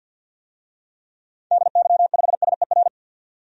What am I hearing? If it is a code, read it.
Morse code, heard as a keyed tone, about 51 wpm, 700 Hz. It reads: DY5SER
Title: DXCC-100 Call Signs - 6 Characters - 50wpm